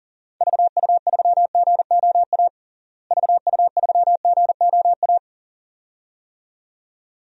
Morse code, heard as a keyed tone, about 40 wpm, 700 Hz. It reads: VU3GOA VU3GOA
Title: DXCC-100 Call Signs - 6 Characters - Repeated 2x - 40wpm